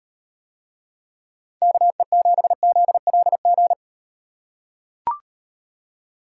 Morse code, heard as a keyed tone, about 38 wpm, 700 Hz. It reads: KE7ZLG